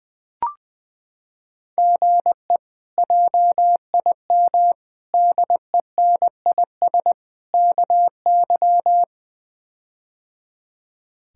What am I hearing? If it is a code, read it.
Morse code, heard as a keyed tone, about 20 wpm, 700 Hz. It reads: GE JIM DENIS KY
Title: K1USN SST QSO Practice - 20wpm